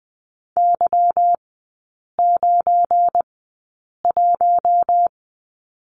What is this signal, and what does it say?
Morse code, heard as a keyed tone, about 20 wpm, spaced as for 8 wpm, 700 Hz. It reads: Y91
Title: 178 Lesson 29e - Slash - Three Characters 20wpm